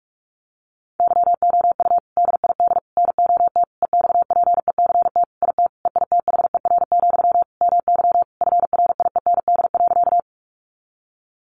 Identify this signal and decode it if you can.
Morse code, heard as a keyed tone, about 44 wpm, 700 Hz. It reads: YOU DID NOT EXPECT IT EITHER, MY FRIEND.
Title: Sentences from Top 1100 Words 45wpm